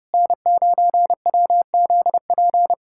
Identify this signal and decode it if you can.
Morse code, heard as a keyed tone, about 30 wpm, 700 Hz. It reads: N9WZP
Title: USA Callsign DE Callsign - 5 Characters - 30wpm